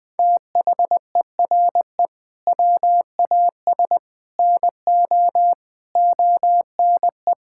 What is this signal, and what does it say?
Morse code, heard as a keyed tone, about 21 wpm, 700 Hz. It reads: THERE WAS NO ONE
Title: Sentences from Top 1600 Words 20wpm